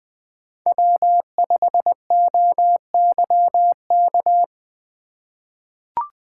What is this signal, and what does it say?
Morse code, heard as a keyed tone, about 20 wpm, 700 Hz. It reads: W5OYK
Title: USA Call Signs - 4x Character Spacing - 20wpm